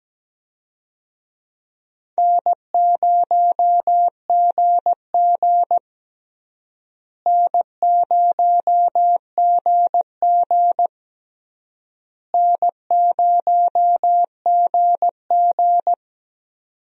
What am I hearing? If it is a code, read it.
Morse code, heard as a keyed tone, about 17 wpm, 700 Hz. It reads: N0GG N0GG N0GG
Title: USA Call Signs - 4 Characters - Repeated 3x - 17wpm